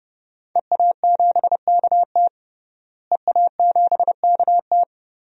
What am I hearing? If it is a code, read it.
Morse code, heard as a keyed tone, about 30 wpm, 700 Hz. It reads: EA7KT EA7KT
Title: DXCC-100 Call Signs - 5 Characters - Repeated 3x - 30wpm